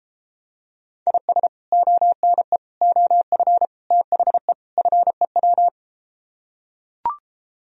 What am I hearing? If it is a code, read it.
Morse code, heard as a keyed tone, about 33 wpm, 700 Hz. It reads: IS ONE OF THE FEW